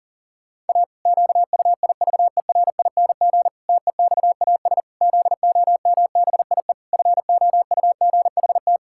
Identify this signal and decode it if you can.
Morse code, heard as a keyed tone, about 40 wpm, 700 Hz. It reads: A QUIVERING TEXAS ZOMBIE FOUGHT
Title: Pangram Challenge v2 - 40wpm